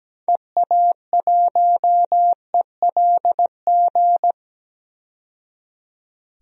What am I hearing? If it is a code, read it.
Morse code, heard as a keyed tone, about 17 wpm, 700 Hz. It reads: EA1ELG
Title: DXCC-100 Call Signs - 6 Characters - 17wpm